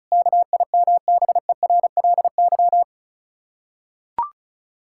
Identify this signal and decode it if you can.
Morse code, heard as a keyed tone, about 36 wpm, 700 Hz. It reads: KIMBERLY
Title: Practice QSO - Exchange 1 - Beginning - 35wpm